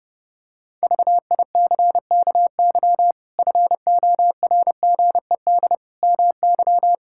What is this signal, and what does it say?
Morse code, heard as a keyed tone, about 30 wpm, 700 Hz. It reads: VICKY FORGED MY
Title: Pangram Challenge v2 - 30wpm